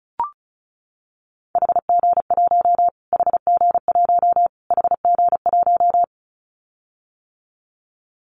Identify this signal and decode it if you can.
Morse code, heard as a keyed tone, about 34 wpm, 700 Hz. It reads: HG1 HG1 HG1
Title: Callsign Prefixes for Top 25 DXCC - Repeated 3x - 35wpm